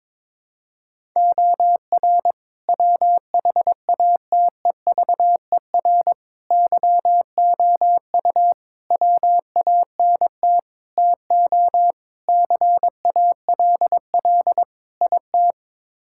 Morse code, heard as a keyed tone, about 21 wpm, 700 Hz. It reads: OR WHATEVER YOU WANT TO CALL IT
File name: practice / 22wpm / Sets of 7 Words - 22wpm